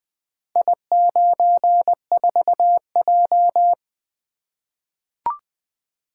Morse code, heard as a keyed tone, about 20 wpm, 700 Hz. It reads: I94J
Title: Four Letter-Number Combinations - 20wpm